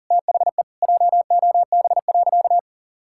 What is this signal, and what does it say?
Morse code, heard as a keyed tone, about 40 wpm, 700 Hz. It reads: THE JOB.
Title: Sentences from Top 1000 Words - Review Part 6 - 40wpm